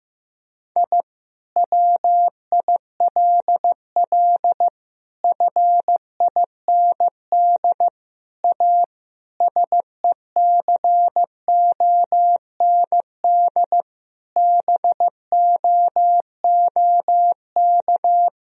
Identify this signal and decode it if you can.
Morse code, heard as a keyed tone, about 15 wpm, 700 Hz. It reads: I WILL FIND A SECOND BOOK
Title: Sentences from Top 1000 Words - Review Part 1 - 15wpm